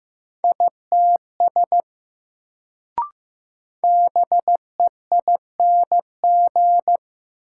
Morse code, heard as a keyed tone, about 15 wpm, 700 Hz. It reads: ITS BEING
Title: Sentences from Top 600 Words - New Word Drill - 15wpm